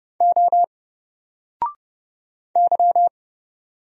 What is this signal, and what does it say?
Morse code, heard as a keyed tone, about 29 wpm, 700 Hz. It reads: O Y
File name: Single-Letter-Number-30wpm